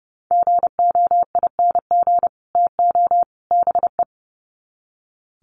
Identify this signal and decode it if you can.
Morse code, heard as a keyed tone, about 30 wpm, 700 Hz. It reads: GOING TO BE
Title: Sets of 3 Words 30wpm